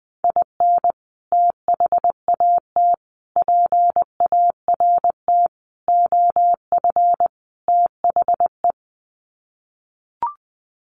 Morse code, heard as a keyed tone, about 20 wpm, 700 Hz. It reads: IN THAT PART OF THE
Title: Sets of 5 Words Encore2 20wpm